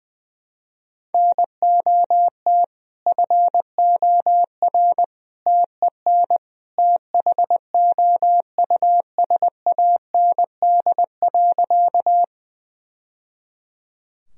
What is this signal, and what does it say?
Morse code, heard as a keyed tone, about 21 wpm, 700 Hz. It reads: NOT FOR TEN THOUSAND.
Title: Sentences from Top 1000 Words - Review Part 3 - 20wpm